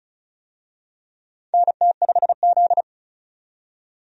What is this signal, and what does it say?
Morse code, heard as a keyed tone, about 36 wpm, 700 Hz. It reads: NT5Z
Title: USA Call Signs - 4 Characters - 35wpm